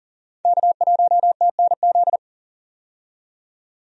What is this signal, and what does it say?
Morse code, heard as a keyed tone, about 40 wpm, 700 Hz. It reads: K1TNZ
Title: USA Call Signs - 5 Characters - 40wpm